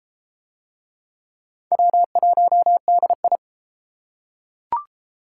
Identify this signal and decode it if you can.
Morse code, heard as a keyed tone, about 28 wpm, 700 Hz. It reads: W1DI
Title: Speed Racing - USA 1x2 Call Signs 22wpm